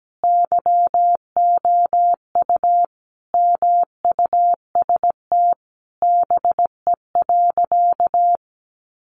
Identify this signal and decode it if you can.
Morse code, heard as a keyed tone, about 17 wpm, 700 Hz. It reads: YOU MUST BE.
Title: Sentences from Top 1400 Words 17wpm